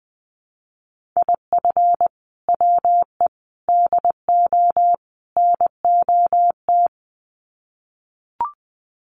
Morse code, heard as a keyed tone, about 20 wpm, 700 Hz. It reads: IF WE DO NOT
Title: Sets of 4 Words 20wpm